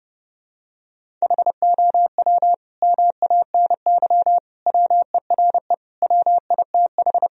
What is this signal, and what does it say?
Morse code, heard as a keyed tone, about 30 wpm, 700 Hz. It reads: HOW MANY WERE WITH
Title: Sentences from Top 1000 Words - Review Part 1 - 30wpm